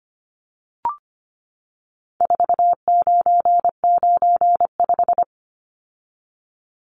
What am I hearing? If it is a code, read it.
Morse code, heard as a keyed tone, about 25 wpm, 700 Hz. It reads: 4995
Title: Numbers - 4-Digit 25wpm